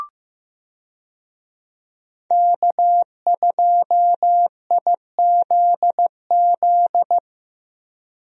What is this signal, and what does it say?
Morse code, heard as a keyed tone, about 15 wpm, 700 Hz. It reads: K2IZZ
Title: USA Call Signs - 5 Characters - Repeated 2x - 15wpm